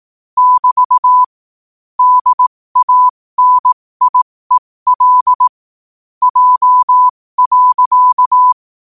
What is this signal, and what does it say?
Morse code, heard as a keyed tone, about 18 wpm, spaced as for 15 wpm, 1000 Hz. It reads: <BT> DANIEL J.
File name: qotdfiles / 15wpm